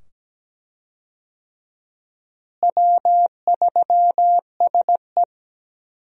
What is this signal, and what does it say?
Morse code, heard as a keyed tone, about 17 wpm, 700 Hz. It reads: W3SE
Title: USA Call Signs - 4 Characters - Repeated 2x - 17wpm